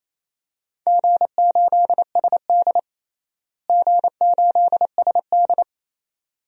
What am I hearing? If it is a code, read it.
Morse code, heard as a keyed tone, about 28 wpm, 700 Hz. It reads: G8SD G8SD
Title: DXCC-100 Call Signs - 4 Characters - Repeated 2x - 28wpm